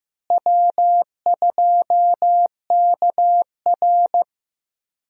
Morse code, heard as a keyed tone, about 15 wpm, 700 Hz. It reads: W2KR
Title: USA Call Signs - 4 Characters - Repeated 2x - 15wpm